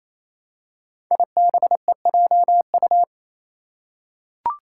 Morse code, heard as a keyed tone, about 27 wpm, 700 Hz. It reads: IBEJU
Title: Five Challenging Letter-Number Combinations 28wpm